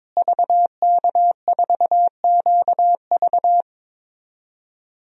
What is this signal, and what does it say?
Morse code, heard as a keyed tone, about 22 wpm, 700 Hz. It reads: VK4QV
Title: DXCC-100 Call Signs - 5 Characters - Repeated 3x - 22wpm